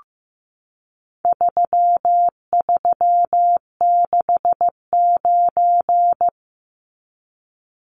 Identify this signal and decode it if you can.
Morse code, heard as a keyed tone, about 15 wpm, 700 Hz. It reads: 3369
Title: Numbers - 4-Digit 15wpm